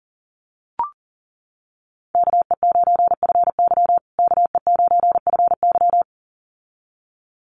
Morse code, heard as a keyed tone, about 40 wpm, 700 Hz. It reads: KE9FY KE9FY
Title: USA Call Signs Repeated 2x 40wpm